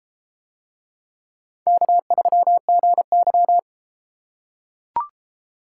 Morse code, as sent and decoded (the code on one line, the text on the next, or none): -.- ...-- --. -.--
K3GY